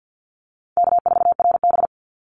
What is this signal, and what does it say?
Morse code, heard as a keyed tone, about 50 wpm, 700 Hz. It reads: K4RB
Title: Easy Call Signs Repeated 2x - 50wpm